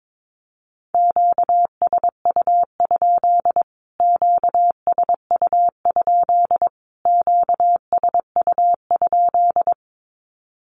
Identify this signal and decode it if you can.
Morse code, heard as a keyed tone, about 22 wpm, 700 Hz. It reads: QSU? QSU? QSU?
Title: Q-Codes Repeated 3x 22wpm